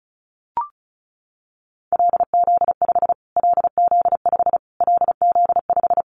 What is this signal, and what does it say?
Morse code, heard as a keyed tone, about 34 wpm, 700 Hz. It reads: LZ5 LZ5 LZ5
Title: Callsign Prefixes for Top 25 DXCC - Repeated 3x - 35wpm